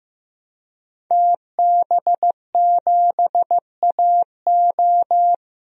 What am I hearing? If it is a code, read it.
Morse code, heard as a keyed tone, about 15 wpm, 700 Hz. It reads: TB7AO
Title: DXCC-100 Call Signs - 5 Characters - 15wpm